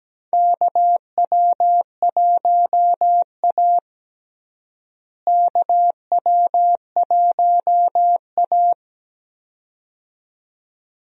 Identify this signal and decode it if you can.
Morse code, heard as a keyed tone, about 17 wpm, 700 Hz. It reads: KW1A KW1A
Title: USA Call Signs - 4 Characters - Repeated 2x - 17wpm